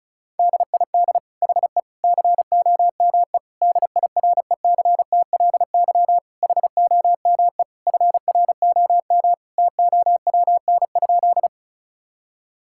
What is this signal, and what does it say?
Morse code, heard as a keyed tone, about 36 wpm, 700 Hz. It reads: DID HE COME DIRECTLY HOME FROM TOWN?
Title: Sentences from Top 1000 Words - Review Part 5 - 35wpm